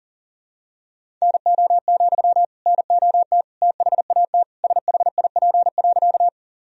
Morse code, heard as a keyed tone, about 40 wpm, 700 Hz. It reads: NO, NOT THAT SHIP.
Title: Sentences from Top 1000 Words - Review Part 2 - 40wpm